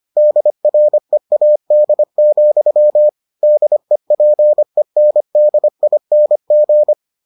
Morse code, heard as a keyed tone, about 25 wpm, 600 Hz. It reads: DREAD, DEPENDING